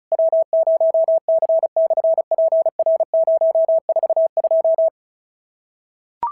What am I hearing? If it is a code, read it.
Morse code, heard as a keyed tone, about 36 wpm, 690 Hz. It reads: W0C/PR042
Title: SOTA QSO Practice 35wpm